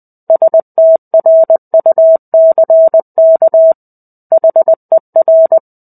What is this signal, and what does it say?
Morse code, heard as a keyed tone, about 20 wpm, 650 Hz. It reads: STRUCK HER
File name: 05-Chapter-05 20 WPM